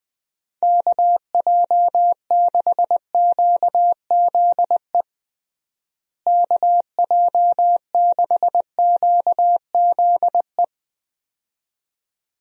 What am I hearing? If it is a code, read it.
Morse code, heard as a keyed tone, about 20 wpm, 700 Hz. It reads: KJ6QZE KJ6QZE
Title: USA Call Signs - 6 Characters - Repeated 2x - 20wpm